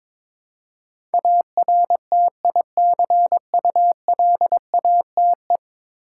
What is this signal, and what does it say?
Morse code, heard as a keyed tone, about 22 wpm, 700 Hz. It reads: ARTICULATE
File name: practice / 22wpm / 10 Letter Words - First 3 Letters Given - 22wpm